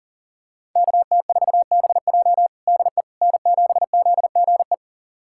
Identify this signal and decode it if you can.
Morse code, heard as a keyed tone, about 37 wpm, 700 Hz. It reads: KT4BJ DE N7ZGE
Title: USA Callsign DE Callsign - 5 Characters - 40wpm